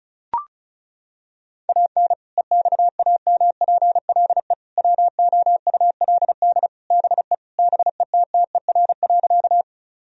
Morse code, heard as a keyed tone, about 36 wpm, 700 Hz. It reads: AN EXAMPLE WOULD BE BETTER.
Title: Sentences from Top 1000 Words - Review Part 1 - 35wpm